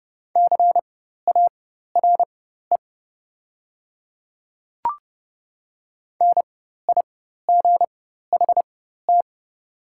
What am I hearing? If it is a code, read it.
Morse code, heard as a keyed tone, about 30 wpm, spaced as for 14 wpm, 700 Hz. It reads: CARE NIGHT
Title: 4x Character Spacing - Top 300 Words - 30wpm